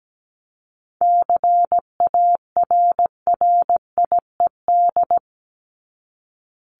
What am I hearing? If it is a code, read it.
Morse code, heard as a keyed tone, about 17 wpm, 700 Hz. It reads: CARRIED
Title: ED Suffix - Top 100 Words with Base Words 17wpm